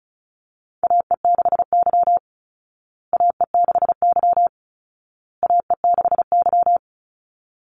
Morse code, heard as a keyed tone, about 34 wpm, 700 Hz. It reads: AE6Y AE6Y AE6Y
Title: Call Signs Repeated 2x - 35wpm